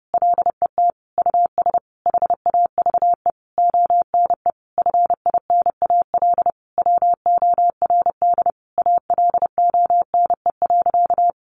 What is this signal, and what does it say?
Morse code, heard as a keyed tone, about 30 wpm, 700 Hz. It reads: LET US HAVE ONE FINAL WORD ALONE.
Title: Sentences from Top 1000 Words 30wpm